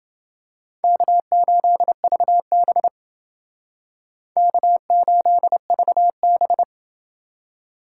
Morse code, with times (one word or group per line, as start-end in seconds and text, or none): 0.8-2.9 K8VB
4.4-6.6 K8VB